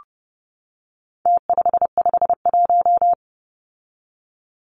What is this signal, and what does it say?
Morse code, heard as a keyed tone, about 30 wpm, 700 Hz. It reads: T551
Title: Numbers - Contest Style - 4-Digit 30wpm